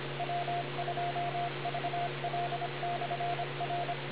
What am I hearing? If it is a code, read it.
Morse code, heard as a keyed tone, about 26 wpm, 700 Hz, with radio noise. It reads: W2VL/R